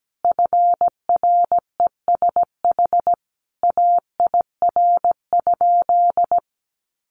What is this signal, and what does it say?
Morse code, heard as a keyed tone, about 17 wpm, 700 Hz. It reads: FRESH AIR?
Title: Sentences from Top 1200 Words 17wpm